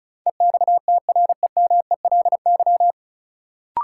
Morse code, heard as a keyed tone, about 34 wpm, 700 Hz. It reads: EXTREMELY
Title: Sentences from Top 1000 Words - Review Part 7 - New Word Drill - 35wpm